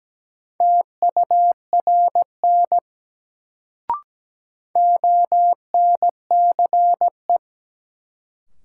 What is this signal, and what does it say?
Morse code, heard as a keyed tone, about 17 wpm, 700 Hz. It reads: TURN ONCE
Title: Sets of 4 Words - New Word Drill - 17wpm